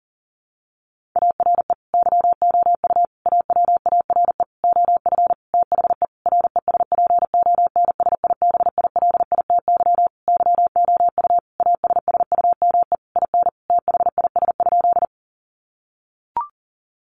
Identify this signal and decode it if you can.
Morse code, heard as a keyed tone, about 40 wpm, 700 Hz. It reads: ARE YOU AWARE OF THE RESPONSIBILITY YOU ASSUME IN THIS?
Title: Sentences from Top 1500 Words 40wpm